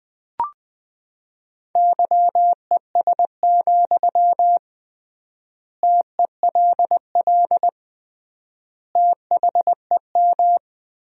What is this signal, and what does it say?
Morse code, heard as a keyed tone, about 20 wpm, 700 Hz. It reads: YES, TELL THEM
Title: 3x Word-Spacing - Sentences from Top 500 Words - 20wpm